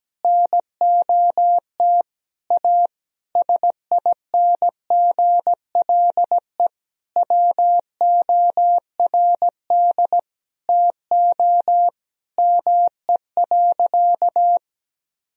Morse code, heard as a keyed tone, about 17 wpm, 700 Hz. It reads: NOT A SINGLE WORD TO ME.
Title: Sentences from Top 1000 Words - Review Part 4 - 17wpm